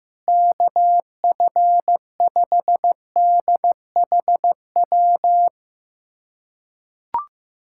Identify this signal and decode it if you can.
Morse code, heard as a keyed tone, about 15 wpm, 700 Hz. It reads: KF5DHW
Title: USA Callsign DE Callsign - 6 Characters - 15wpm